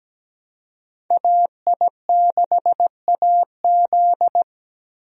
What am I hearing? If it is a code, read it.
Morse code, heard as a keyed tone, about 17 wpm, 700 Hz. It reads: AI6AZ
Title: USA Call Signs - 5 Characters - Repeated 2x - 17wpm